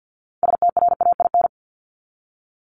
Morse code, heard as a keyed tone, about 48 wpm, 700 Hz. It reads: STRAIN
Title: Top 1751-2000 Words 50wpm